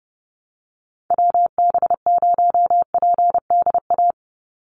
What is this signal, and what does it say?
Morse code, heard as a keyed tone, about 30 wpm, 700 Hz. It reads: WB0PDA
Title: USA Call Signs 30wpm